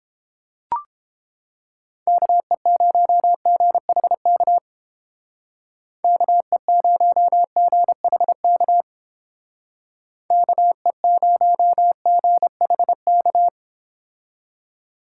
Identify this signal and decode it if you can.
Morse code, heard as a keyed tone, about 29 wpm, 700 Hz. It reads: KE0GHK KE0GHK KE0GHK